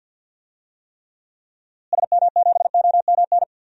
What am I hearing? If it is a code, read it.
Morse code, heard as a keyed tone, about 50 wpm, 700 Hz. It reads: SM7OMN
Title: DXCC-100 Call Signs - 6 Characters - Repeated 3x - 50wpm